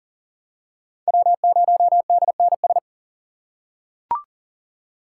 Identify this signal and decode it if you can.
Morse code, heard as a keyed tone, about 40 wpm, 700 Hz. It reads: W0DNS
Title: USA Call Signs - 5 Characters - 4x Character Spacing - 40wpm